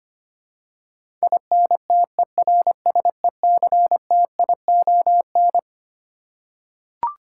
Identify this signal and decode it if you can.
Morse code, heard as a keyed tone, about 25 wpm, 700 Hz. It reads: INTERSECTION
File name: practice / 25wpm / Twelve Letter Words - 25wpm